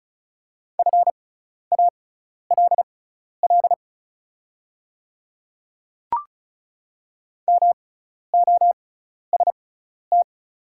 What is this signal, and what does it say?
Morse code, heard as a keyed tone, about 36 wpm, spaced as for 12 wpm, 700 Hz. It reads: FALL MOST
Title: 6x Character Spacing - Top 500 Words - 35wpm